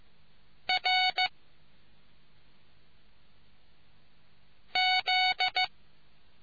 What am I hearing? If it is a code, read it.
Morse code, heard as a keyed tone, about 15 wpm, 740 Hz. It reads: R Z